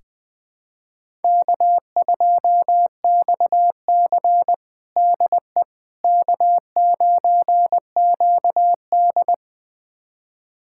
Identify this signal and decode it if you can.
Morse code, heard as a keyed tone, about 21 wpm, 700 Hz. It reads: K2XC DE K9QD
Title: USA Callsign DE Callsign - 4 Characters - 20wpm